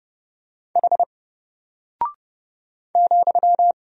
Morse code, heard as a keyed tone, about 30 wpm, 700 Hz. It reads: H ,
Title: Single Character (Letter:Number:Punctuation:Prosign) Alt 30wpm